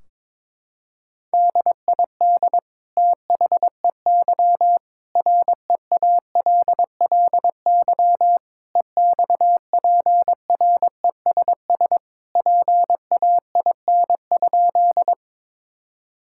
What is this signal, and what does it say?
Morse code, heard as a keyed tone, about 22 wpm, 700 Hz. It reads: DID THEY REALLY EXPRESS PAIN?
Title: Sentences from Top 1600 Words 22wpm